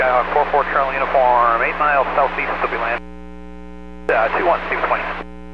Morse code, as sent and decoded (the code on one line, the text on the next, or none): none